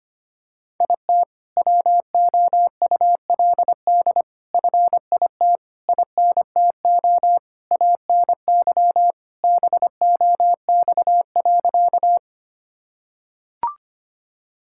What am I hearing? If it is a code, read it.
Morse code, heard as a keyed tone, about 25 wpm, 700 Hz. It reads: IT WOULD FIT INTO ANY BOX.
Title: Sentences from Top 1000 Words - Review Part 5 - 25wpm